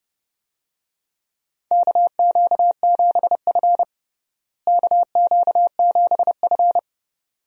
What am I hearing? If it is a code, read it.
Morse code, heard as a keyed tone, about 32 wpm, 700 Hz. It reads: KQ7F KQ7F
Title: USA Call Signs - 4 Characters - Repeated 2x - 30wpm